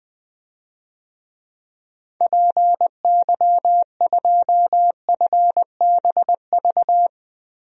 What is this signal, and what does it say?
Morse code, heard as a keyed tone, about 20 wpm, 700 Hz. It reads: PY2FBV
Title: DXCC-100 Call Signs - 6 Characters - 20wpm